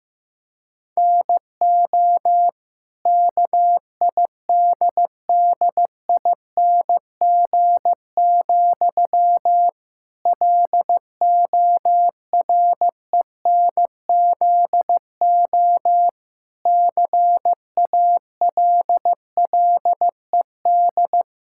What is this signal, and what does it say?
Morse code, heard as a keyed tone, about 15 wpm, 700 Hz. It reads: NO KIDDING, LORENZO CALLED
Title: Pangram Challenge v2 - 15wpm